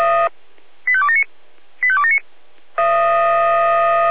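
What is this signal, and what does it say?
Five-tone selective call (ZVEI), heard as quick runs of five tones, tones of about 70 ms. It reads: 85279 85279
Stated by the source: Received on an SDR.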